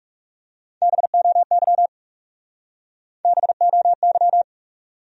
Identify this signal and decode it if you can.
Morse code, heard as a keyed tone, about 39 wpm, 700 Hz. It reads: BOY BOY